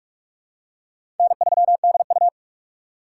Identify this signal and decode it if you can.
Morse code, heard as a keyed tone, about 44 wpm, 700 Hz. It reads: N3DU
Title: USA Call Signs - 6x Character Spacing - 45wpm